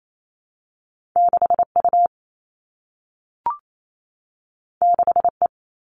Morse code, heard as a keyed tone, about 28 wpm, 700 Hz. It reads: 6U 6E
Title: All Cut Numbers - 2-Digit 28wpm